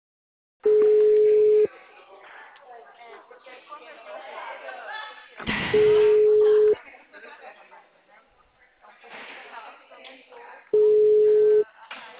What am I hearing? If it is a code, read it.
Ringback tone (Europe, 425 Hz), with speech.
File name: out- -2001- - - .27482